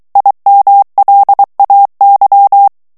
Morse code, heard as a keyed tone, about 23 wpm, 800 Hz. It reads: IMLAY
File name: ANGLAIS texte clair 3 vitesse 22 mots minute